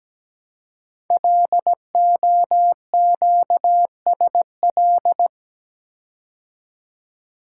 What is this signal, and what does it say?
Morse code, heard as a keyed tone, about 17 wpm, 700 Hz. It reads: LOQSL
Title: Code Groups - 5 Letters - 17wpm